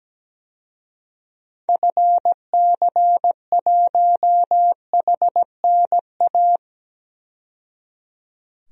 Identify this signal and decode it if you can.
Morse code, heard as a keyed tone, about 17 wpm, 700 Hz. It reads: FC1HNA
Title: DXCC-100 Call Signs - 6 Characters - 17wpm